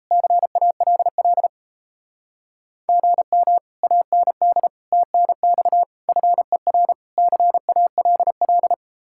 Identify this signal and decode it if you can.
Morse code, heard as a keyed tone, about 38 wpm, 700 Hz. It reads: CALL GM AND TNX FER CALL